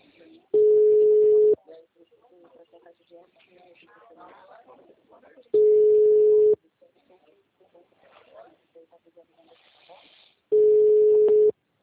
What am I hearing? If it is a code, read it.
Ringback tone (Europe, 425 Hz), with speech.